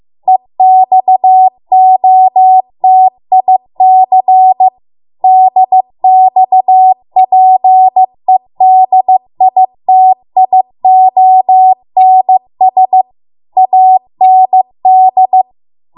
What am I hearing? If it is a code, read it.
Morse code, heard as a keyed tone, about 15 wpm, 750 Hz. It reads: EXOTIC DXPEDITIONS AND